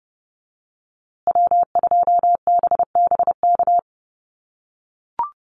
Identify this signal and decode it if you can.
Morse code, heard as a keyed tone, about 30 wpm, 700 Hz. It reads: W2BBK
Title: USA Call Signs Repeated 2x 30wpm